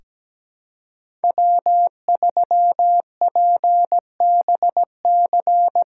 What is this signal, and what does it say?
Morse code, heard as a keyed tone, about 17 wpm, 700 Hz. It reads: W3PBC
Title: USA Callsign DE Callsign 17wpm